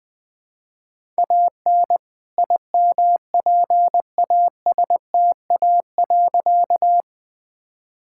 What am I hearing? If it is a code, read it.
Morse code, heard as a keyed tone, about 20 wpm, 700 Hz. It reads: AN IMPASTA.
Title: All-Age Appropriate Jokes - 20wpm